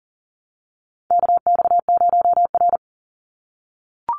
Morse code, heard as a keyed tone, about 40 wpm, 700 Hz. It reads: KX0R
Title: NA SOTA Activator Callsigns 40wpm